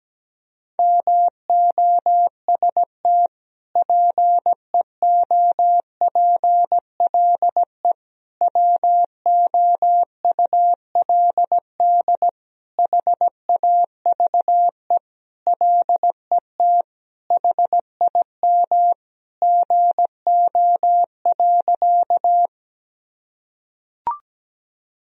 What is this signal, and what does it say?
Morse code, heard as a keyed tone, about 17 wpm, 700 Hz. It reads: MOST PEOPLE WOULD HAVE LET HIM GO.
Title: Sentences from Top 1000 Words - Review Part 1 - 17wpm